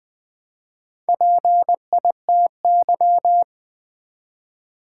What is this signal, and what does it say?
Morse code, heard as a keyed tone, about 20 wpm, 700 Hz. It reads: PITY
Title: Build A Word - 1 Letter At A Time - 4-Letter Words - 20wpm